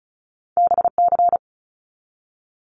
Morse code, heard as a keyed tone, about 36 wpm, 700 Hz. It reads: BC
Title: Canadian Provinces and Territories 35wpm